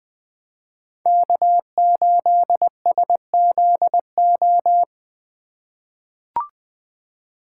Morse code, heard as a keyed tone, about 20 wpm, 700 Hz. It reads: K8SZO
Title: USA Call Signs - 5 Characters - 4x Character Spacing - 20wpm